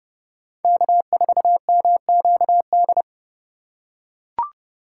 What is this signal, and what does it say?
Morse code, heard as a keyed tone, about 28 wpm, 700 Hz. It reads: K4MQD